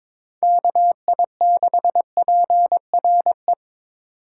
Morse code, heard as a keyed tone, about 22 wpm, 700 Hz. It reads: KI6PRE
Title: USA Call Signs - 6 Characters - 3x Character Spacing - 22wpm